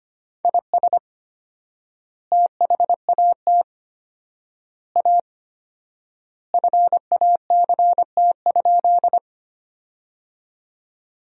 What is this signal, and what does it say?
Morse code, heard as a keyed tone, about 25 wpm, 700 Hz. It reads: IS THAT A FACT?
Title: 4x Word-Spacing - Sentences from Top 500 Words - 25wpm